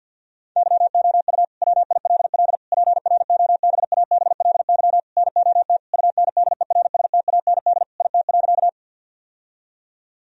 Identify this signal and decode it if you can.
Morse code, heard as a keyed tone, about 50 wpm, 700 Hz. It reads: YOU WILL PROBABLY NOT UNDERSTAND IT.
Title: Sentences from Top 1000 Words - Review Part 4 - 50wpm